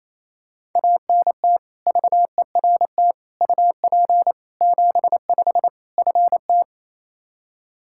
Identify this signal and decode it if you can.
Morse code, heard as a keyed tone, about 28 wpm, 700 Hz. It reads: ANT VERT UP 75 FT
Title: K1USN SST QSO Practice 28wpm